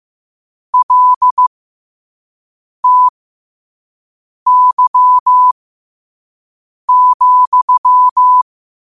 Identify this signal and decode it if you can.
Morse code, heard as a keyed tone, about 15 wpm, spaced as for 5 wpm, 1000 Hz. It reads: LTY,
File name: qotdfiles / 5wpm